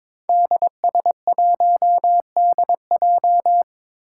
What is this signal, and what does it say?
Morse code, heard as a keyed tone, about 22 wpm, 700 Hz. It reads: DS1DJ
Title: Five Challenging Letter-Number Combinations 22wpm